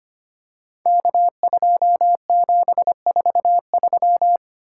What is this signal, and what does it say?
Morse code, heard as a keyed tone, about 25 wpm, 700 Hz. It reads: K2743
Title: Code Groups - 5 Letter:Numbers - 25wpm